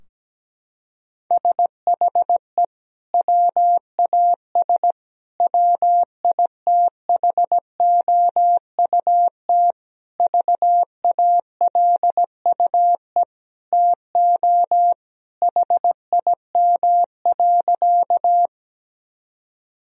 Morse code, heard as a keyed tone, about 17 wpm, 700 Hz. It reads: SHE WAS WITHOUT VALUE TO HIM.
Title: Sentences from Top 1000 Words - Review Part 3 - 17wpm